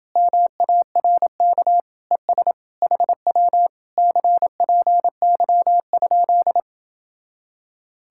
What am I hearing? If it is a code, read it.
Morse code, heard as a keyed tone, about 29 wpm, 700 Hz. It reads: MARK ES HW CPY?